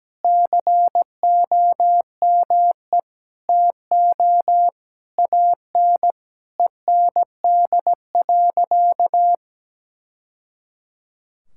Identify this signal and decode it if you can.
Morse code, heard as a keyed tone, about 17 wpm, 700 Hz. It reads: COME TO AN END.
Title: Sentences from Top 1000 Words - Review Part 6 - 17wpm